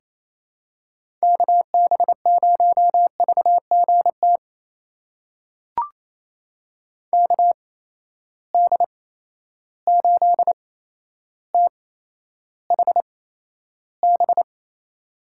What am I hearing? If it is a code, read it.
Morse code, heard as a keyed tone, about 28 wpm, spaced as for 12 wpm, 700 Hz. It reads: KB0VGT KD8THB
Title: USA Call Signs - 6 Characters - 8x Character Spacing - 28wpm